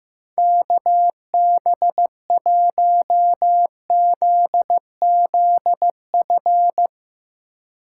Morse code, heard as a keyed tone, about 15 wpm, 700 Hz. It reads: KB1ZZF
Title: USA Call Signs - 4x Character Spacing - 15wpm